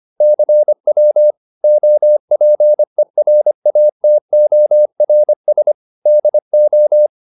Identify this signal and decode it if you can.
Morse code, heard as a keyed tone, about 25 wpm, 600 Hz. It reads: CW OPERATORS DO